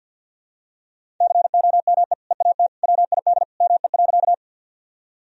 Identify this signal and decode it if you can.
Morse code, heard as a keyed tone, about 52 wpm, 700 Hz. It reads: COME EAT WID ME.
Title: Sentences from Top 300 Words and Abbreviations 50wpm